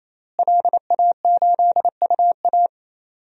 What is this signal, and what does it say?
Morse code, heard as a keyed tone, about 28 wpm, 700 Hz. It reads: LA8UA
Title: DXCC-100 Call Signs - 5 Characters - Repeated 3x - 28wpm